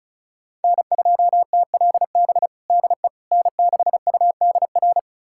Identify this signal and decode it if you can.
Morse code, heard as a keyed tone, about 36 wpm, 700 Hz. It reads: N2TLB DE N6UDR
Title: USA Callsign DE Callsign - 5 Characters - 35wpm